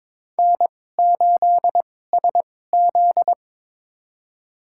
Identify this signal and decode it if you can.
Morse code, heard as a keyed tone, about 22 wpm, spaced as for 16 wpm, 700 Hz. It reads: N8SZ
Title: USA Call Signs - 4 Characters - 2x Character Spacing - 22wpm